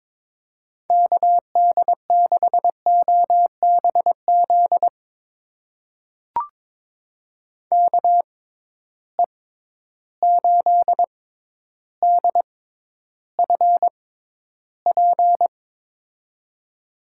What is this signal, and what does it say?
Morse code, heard as a keyed tone, about 22 wpm, spaced as for 11 wpm, 700 Hz. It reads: KD6OBZ KE8DFP
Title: USA Call Signs - 6 Characters - 6x Character Spacing - 22wpm